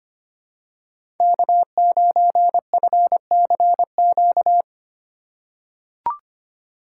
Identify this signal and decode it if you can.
Morse code, heard as a keyed tone, about 25 wpm, 700 Hz. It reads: K9FCQ
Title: USA Call Signs - 3x Character Spacing - 25wpm